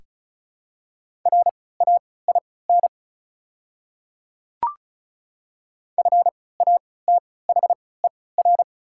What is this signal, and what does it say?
Morse code, heard as a keyed tone, about 36 wpm, spaced as for 20 wpm, 690 Hz. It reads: RAIN FATHER
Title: 3x Character Spacing - Top 300 Words - 35wpm